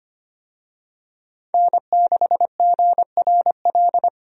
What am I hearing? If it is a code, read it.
Morse code, heard as a keyed tone, about 25 wpm, 700 Hz. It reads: N6GRL
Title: USA Call Signs - 5 Characters - 25wpm